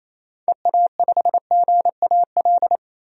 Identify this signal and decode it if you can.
Morse code, heard as a keyed tone, about 28 wpm, 700 Hz. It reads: EA5GAL
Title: DXCC-100 Call Signs - 6 Characters - 28wpm